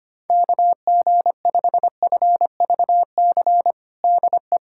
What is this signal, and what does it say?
Morse code, heard as a keyed tone, about 25 wpm, 700 Hz. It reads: KG5FVC DE
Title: USA Callsign DE Callsign 25wpm